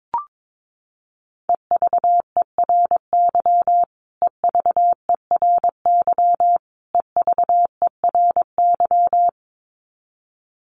Morse code, heard as a keyed tone, about 22 wpm, 700 Hz. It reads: EVERY EVERY EVERY
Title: Top 500 Words Repeated 22wpm